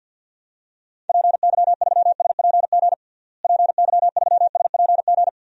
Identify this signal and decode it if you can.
Morse code, heard as a keyed tone, about 52 wpm, 700 Hz. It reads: PY3SPG PY3SPG
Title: DXCC-100 Call Signs - 6 Characters - Repeated 2x - 50wpm